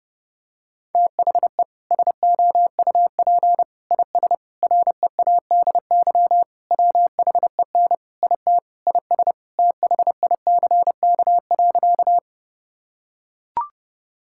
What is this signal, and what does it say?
Morse code, heard as a keyed tone, about 30 wpm, 700 Hz. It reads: THE SOUP IS READY WHEN IT IS THICK.
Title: Sentences from Top 1600 Words 30wpm